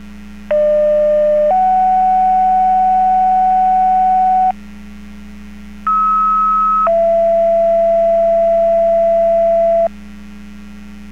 Two-tone page, heard as a one-second tone, then a three-second tone, about 600 then 750 Hz, about 1290 then 670 Hz. It is heard from 0.5 to 9.9 s.